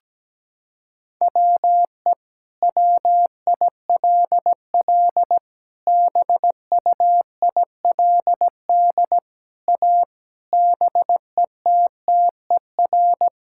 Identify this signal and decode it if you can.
Morse code, heard as a keyed tone, about 17 wpm, 700 Hz. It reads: WE WILL BUILD A BETTER
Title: Sentences from Top 1000 Words - Review Part 6 - 17wpm